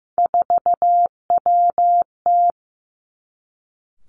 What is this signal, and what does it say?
Morse code, heard as a keyed tone, about 15 wpm, 700 Hz. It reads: 4WT
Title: Three Challenging Letter-Number Combinations 15wpm